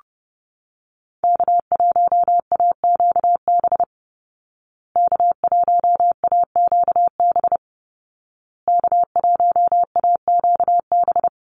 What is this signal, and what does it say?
Morse code, heard as a keyed tone, about 30 wpm, 700 Hz. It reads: K1AQB K1AQB K1AQB
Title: Call Signs Repeated 2x - 30wpm